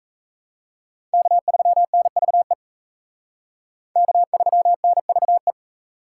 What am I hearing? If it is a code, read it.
Morse code, heard as a keyed tone, about 34 wpm, 700 Hz. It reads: K3NVE K3NVE
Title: Speed Racing - USA 1x3 Call Signs 28wpm